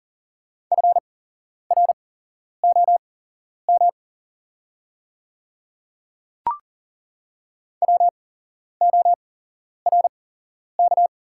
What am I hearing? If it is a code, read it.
Morse code, heard as a keyed tone, about 40 wpm, spaced as for 11 wpm, 690 Hz. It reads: FROM WORK
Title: 8x Character Spacing - Top 300 Words - 40wpm